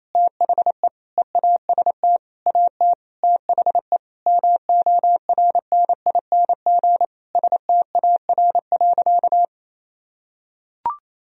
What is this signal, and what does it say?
Morse code, heard as a keyed tone, about 28 wpm, 700 Hz. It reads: THE EAST AT THE MORNING STAR.
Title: Sentences from Top 1000 Words - Review Part 3 - 28wpm